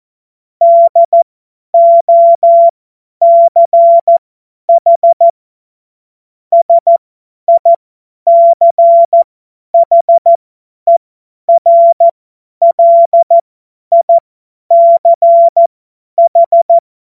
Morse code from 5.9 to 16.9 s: SICHERLICH